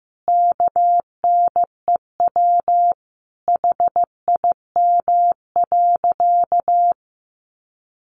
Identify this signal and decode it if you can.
Morse code, heard as a keyed tone, about 15 wpm, 700 Hz. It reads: KNEW HIM.
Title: Sentences from Top 1500 Words 15wpm